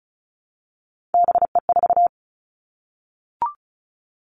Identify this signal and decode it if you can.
Morse code, heard as a keyed tone, about 36 wpm, 700 Hz. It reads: BE4
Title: Three Challenging Letter-Number Combinations 35wpm